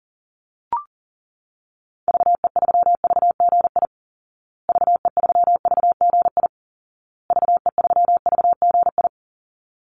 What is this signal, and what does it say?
Morse code, heard as a keyed tone, about 40 wpm, 700 Hz. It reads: VE3VGI VE3VGI VE3VGI
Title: Call Signs Repeated 2x - 40wpm